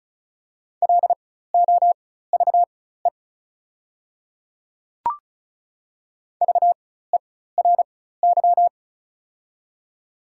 Morse code, heard as a keyed tone, about 36 wpm, spaced as for 16 wpm, 690 Hz. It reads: LOVE VERY
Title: 4x Character Spacing - Top 500 Words - 35wpm